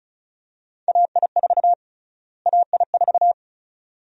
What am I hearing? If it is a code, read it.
Morse code, heard as a keyed tone, about 34 wpm, 700 Hz. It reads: AI4 AI4
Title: USA Call Sign Prefixes - Repeated 2x - 35wpm